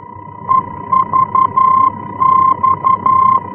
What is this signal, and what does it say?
Morse code, heard as a keyed tone, about 11 wpm, 1050 Hz. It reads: EVX
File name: france / mx / EVX-F-VOR